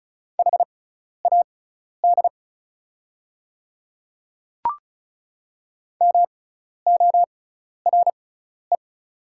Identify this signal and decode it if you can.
Morse code, heard as a keyed tone, about 36 wpm, spaced as for 12 wpm, 700 Hz. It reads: HAD MORE
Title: 6x Character Spacing - Top 100 Words - 35wpm